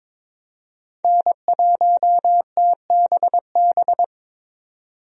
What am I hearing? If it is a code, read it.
Morse code, heard as a keyed tone, about 22 wpm, 700 Hz. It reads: N1TBB
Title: USA Call Signs - 5 Characters - 22wpm